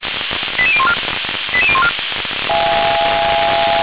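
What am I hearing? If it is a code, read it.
Five-tone selective call (ZVEI), heard as quick runs of five tones, tones of about 70 ms. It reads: 88014 88014
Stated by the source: ZVEI five-tone selective call received with an SDR.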